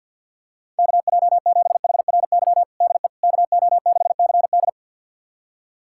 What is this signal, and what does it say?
Morse code, heard as a keyed tone, about 51 wpm, 700 Hz. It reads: KJ7HRY DE KO6CD
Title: USA Callsign DE Callsign 50wpm